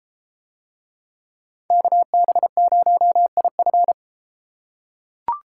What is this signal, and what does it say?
Morse code, heard as a keyed tone, about 33 wpm, 700 Hz. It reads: KB0IF